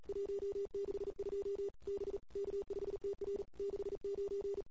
Morse code, heard as a keyed tone, about 34 wpm, 410 Hz. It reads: 162 B K5TR 69
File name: CW / 2008-ss-cw-k5tr-op-n5rz -22:00